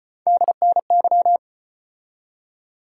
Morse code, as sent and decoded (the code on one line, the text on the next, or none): -.. -. -.--
DNY